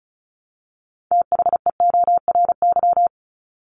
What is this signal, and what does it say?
Morse code, heard as a keyed tone, about 34 wpm, 700 Hz. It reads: THEORY
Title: Top 501-750 Words 35wpm